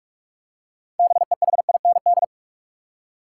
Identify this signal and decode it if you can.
Morse code, heard as a keyed tone, about 46 wpm, 700 Hz. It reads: BEHIND
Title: Six Letter Words - 45wpm